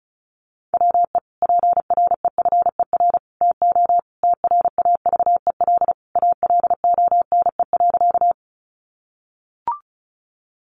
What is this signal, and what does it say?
Morse code, heard as a keyed tone, about 34 wpm, 700 Hz. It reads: WE PREFER TO TRAVEL ALONE.
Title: Sentences from Top 1100 Words 35wpm